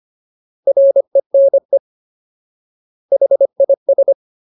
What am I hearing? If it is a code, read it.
Morse code, heard as a keyed tone, about 25 wpm, 550 Hz. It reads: RENE HIS
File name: Ultimate Word List / 25WPM Ultimate